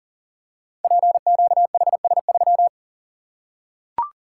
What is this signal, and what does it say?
Morse code, heard as a keyed tone, about 41 wpm, 700 Hz. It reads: PQHS3
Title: Code Groups - 5 Letter:Numbers - 40wpm